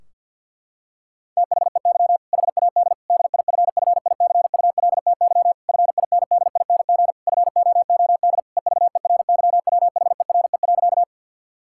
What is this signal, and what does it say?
Morse code, heard as a keyed tone, about 51 wpm, 700 Hz. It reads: THEY HAD DIFFICULTY FINDING FOOD EVERYWHERE.
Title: Sentences from Top 1600 Words 50wpm